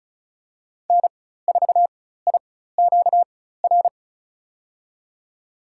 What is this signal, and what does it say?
Morse code, heard as a keyed tone, about 34 wpm, spaced as for 16 wpm, 700 Hz. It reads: N4IQR
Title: USA Call Signs - 5 Characters - 4x Character Spacing - 35wpm